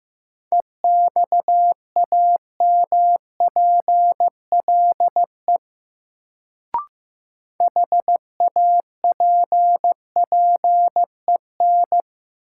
Morse code, heard as a keyed tone, about 15 wpm, 700 Hz. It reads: EXAMPLE HAPPEN
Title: Sentences from Top 400 Words - New Word Drill - 15wpm